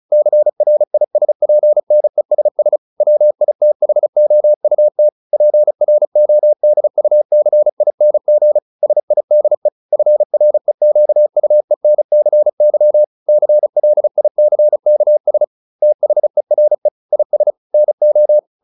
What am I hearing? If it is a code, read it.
Morse code, heard as a keyed tone, about 34 wpm, 600 Hz. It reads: CRISPNESS WITHOUT PRODUCING SIDE FREQUENCY CLICKS THERE IS NO